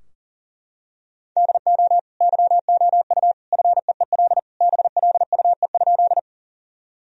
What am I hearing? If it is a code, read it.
Morse code, heard as a keyed tone, about 40 wpm, 700 Hz. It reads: DO YOU FEEL BLUE?
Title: Sentences from Top 1000 Words - Review Part 3 - 40wpm